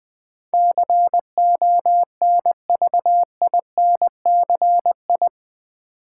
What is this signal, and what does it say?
Morse code, heard as a keyed tone, about 20 wpm, 700 Hz. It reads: CONVINCI
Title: Build A Word - 4 Letters At A Time - 10-Letter Words - 20wpm